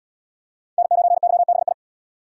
Morse code, heard as a keyed tone, about 76 wpm, 700 Hz. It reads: N9YBI